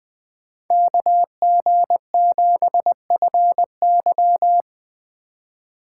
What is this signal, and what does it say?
Morse code, heard as a keyed tone, about 19 wpm, 700 Hz. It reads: KG7FY